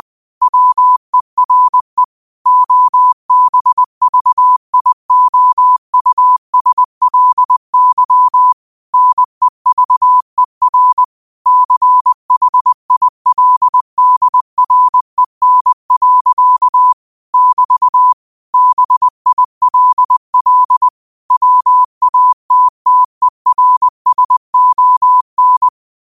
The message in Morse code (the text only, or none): WERE OBVIOUSLY NEVER CHILDREN. <BT> BILL WATTERSON